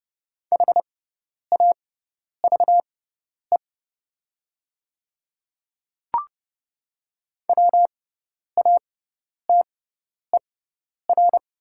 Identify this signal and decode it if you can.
Morse code, heard as a keyed tone, about 30 wpm, spaced as for 10 wpm, 700 Hz. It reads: HAVE WATER
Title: 6x Character Spacing - Top 100 Words - 30wpm